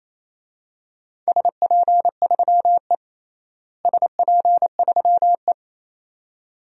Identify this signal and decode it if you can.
Morse code, heard as a keyed tone, about 27 wpm, 700 Hz. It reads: SP3E SP3E
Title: DXCC-100 Call Signs - 4 Characters - Repeated 2x - 28wpm